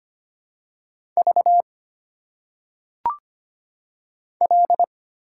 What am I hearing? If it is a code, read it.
Morse code, heard as a keyed tone, about 25 wpm, 700 Hz. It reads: V L
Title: Single Letters - Phonetic Pronunciation 25wpm